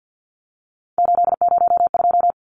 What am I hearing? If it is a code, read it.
Morse code, heard as a keyed tone, about 48 wpm, 700 Hz. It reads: 802
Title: Numbers - 3-Digit 50wpm